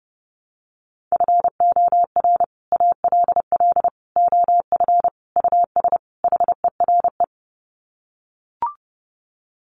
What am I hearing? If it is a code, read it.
Morse code, heard as a keyed tone, about 30 wpm, 700 Hz. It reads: FOR ALL OF US HERE
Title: Sets of 5 Words 30wpm